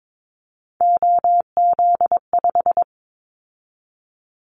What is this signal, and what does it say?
Morse code, heard as a keyed tone, about 22 wpm, 700 Hz. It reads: OZ5
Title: Callsign Prefixes for Top 50 DXCC - 22wpm